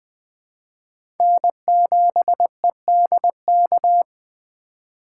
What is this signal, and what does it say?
Morse code, heard as a keyed tone, about 20 wpm, 700 Hz. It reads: N7EDK
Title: 2023 SOTA Chaser Call Signs - 20wpm